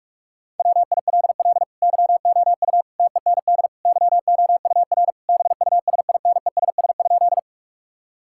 Morse code, heard as a keyed tone, about 46 wpm, 700 Hz. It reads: WILL YOU TEND YOUR BUSINESS?
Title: Sentences from Top 1000 Words - Review Part 5 - 45wpm